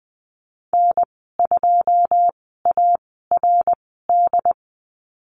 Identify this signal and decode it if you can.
Morse code, heard as a keyed tone, about 20 wpm, spaced as for 15 wpm, 700 Hz. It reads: N2ARD
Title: 082 Lesson 14g - Number 2 - Call Signs 20wpm